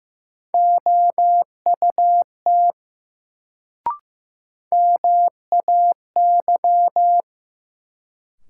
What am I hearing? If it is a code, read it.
Morse code, heard as a keyed tone, about 15 wpm, 700 Hz. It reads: OUT MAY
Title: Sentences from Top 100 Words - New Word Drill - 15wpm